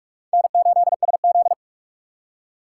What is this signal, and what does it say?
Morse code, heard as a keyed tone, about 43 wpm, 700 Hz. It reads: N8SZ
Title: USA Call Signs - 4 Characters - 3x Character Spacing - 45wpm